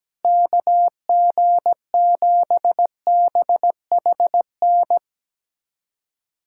Morse code, heard as a keyed tone, about 17 wpm, 700 Hz. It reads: KG7BHN
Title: USA Callsign DE Callsign - 6 Characters - 17wpm